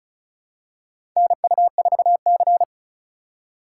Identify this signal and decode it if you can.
Morse code, heard as a keyed tone, about 36 wpm, 700 Hz. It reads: NU4C
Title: USA Call Signs - 4 Characters - 4x Character Spacing - 35wpm